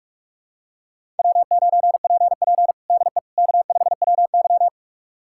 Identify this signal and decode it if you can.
Morse code, heard as a keyed tone, about 46 wpm, 700 Hz. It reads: W9PP DE K5WY
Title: USA Callsign DE Callsign - 4 Characters - 45wpm